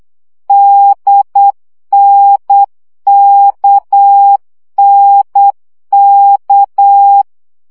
Morse code, heard as a keyed tone, about 8 wpm, 800 Hz. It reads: DNKNK
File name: CW gyakorlas n d k 7 baud